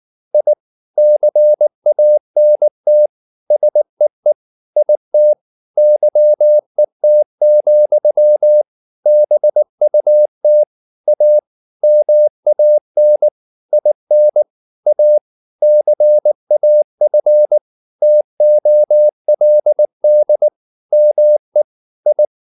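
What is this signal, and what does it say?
Morse code, heard as a keyed tone, about 19 wpm, 600 Hz. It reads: I CANT SEE IT YET, BUT A MAN IN A CAF TOLD ME I
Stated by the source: Morse practice at 18 wpm, English text, from CW med Gnister.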